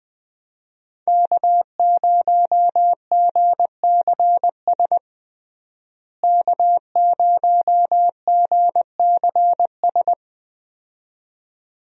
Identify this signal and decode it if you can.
Morse code, heard as a keyed tone, about 20 wpm, 700 Hz. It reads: K0GCS K0GCS
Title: USA Call Signs - 5 Characters - Repeated 2x - 20wpm